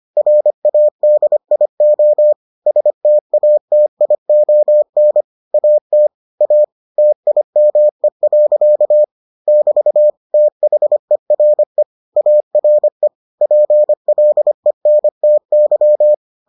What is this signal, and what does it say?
Morse code, heard as a keyed tone, about 25 wpm, 600 Hz. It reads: RADIO STATION AT A TIME. <BT> THERE ARE PLENTY